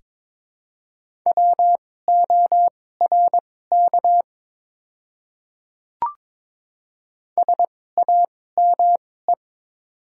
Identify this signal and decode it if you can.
Morse code, heard as a keyed tone, about 22 wpm, spaced as for 16 wpm, 700 Hz. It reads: WORK SAME
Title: 2x Character Spacing - Top 200 Words - 22wpm